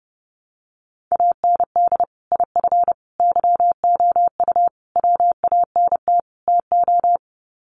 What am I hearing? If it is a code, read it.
Morse code, heard as a keyed tone, about 30 wpm, 700 Hz. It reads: AND IF YOU WANT TO
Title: Sets of 5 Words Encore 30wpm